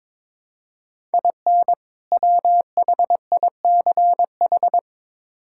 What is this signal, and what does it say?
Morse code, heard as a keyed tone, about 22 wpm, 700 Hz. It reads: IN WHICH
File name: Build A Phrase - 2-Word Phrases - 22wpm